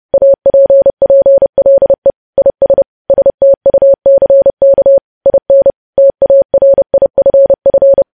Morse code, heard as a keyed tone, about 30 wpm, 550 Hz. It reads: APPLE IS STUCK IN TARIFF
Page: news headlines in morse code at 30 wpm